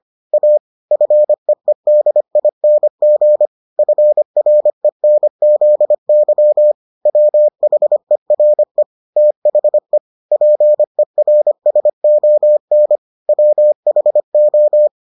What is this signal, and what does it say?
Morse code, heard as a keyed tone, about 25 wpm, 600 Hz. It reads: A FEEDING FRENZY WHERE THE PERSON WHO